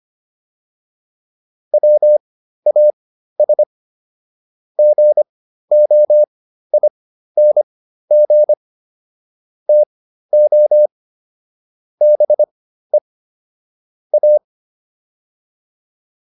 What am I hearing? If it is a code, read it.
Morse code, heard as a keyed tone, about 25 wpm, spaced as for 13 wpm, 600 Hz. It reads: WAS GOING TO BE A